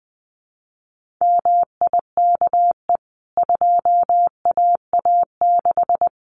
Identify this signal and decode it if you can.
Morse code, heard as a keyed tone, about 20 wpm, 700 Hz. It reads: MIKE 2AA6
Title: CWT Exchange 20wpm